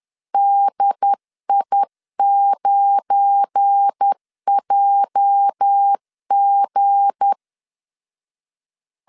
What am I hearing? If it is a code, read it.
Morse code, heard as a keyed tone, about 10 wpm, 800 Hz. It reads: DI9JG
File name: CW / DET / 1 9 1500C S C 10wpm 4557